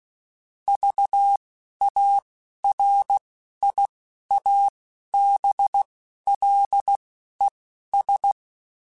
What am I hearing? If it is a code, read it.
Morse code, heard as a keyed tone, about 16 wpm, spaced as for 11 wpm, 800 Hz. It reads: VARIABLES
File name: Technical Subjects / Technical Subjects-mp3s / AND Gate @10